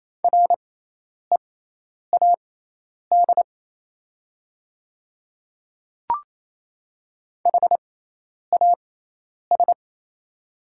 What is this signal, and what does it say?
Morse code, heard as a keyed tone, about 28 wpm, spaced as for 10 wpm, 700 Hz. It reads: READ HAS
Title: 6x Character Spacing - Top 200 Words - 28wpm